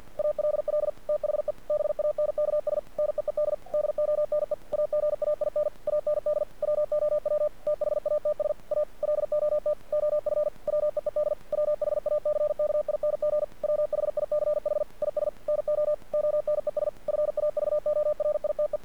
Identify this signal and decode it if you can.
Morse code, heard as a keyed tone, about 48 wpm, 600 Hz. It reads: ALL THE BANKS NEED DONE AGAIN AND WOW THATS A LOT OF WEED WHACKING WHICH IS NO ONES FAVORITE
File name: archived / CRSnet